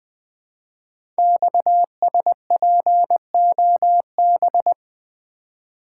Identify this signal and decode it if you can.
Morse code, heard as a keyed tone, about 20 wpm, 700 Hz. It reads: XSPOB
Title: Code Groups - 5 Letters - 20wpm